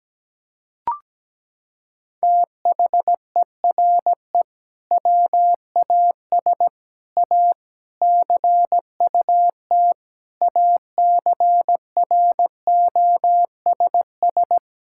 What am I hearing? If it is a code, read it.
Morse code, heard as a keyed tone, about 17 wpm, 700 Hz. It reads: THERE WAS A CUT ACROSS
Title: Sentences from Top 1000 Words - Review Part 5 - 17wpm